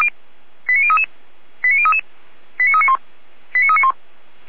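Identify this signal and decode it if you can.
Five-tone selective call (ZVEI), heard as quick runs of five tones, tones of about 70 ms. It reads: 89033 89033 89391 89391
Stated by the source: ZVEI five-tone selective call 89502 received with an SDR.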